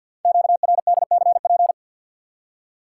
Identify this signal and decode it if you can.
Morse code, heard as a keyed tone, about 53 wpm, 700 Hz. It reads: QRDCP
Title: Code Groups - 5 Letter:Numbers - 50wpm